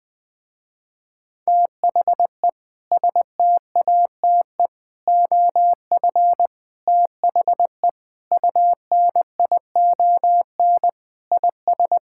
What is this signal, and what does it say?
Morse code, heard as a keyed tone, about 20 wpm, 700 Hz. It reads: THE STATE OF THE UNION IS
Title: Sentences with 3-Word Phrases - Spoken Beginning - 20wpm